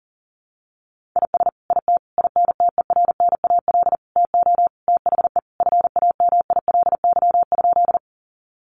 Morse code, heard as a keyed tone, about 40 wpm, 700 Hz. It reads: IS IT INTERNAL TO THE FAMILY?
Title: Sentences from Top 1400 Words 40wpm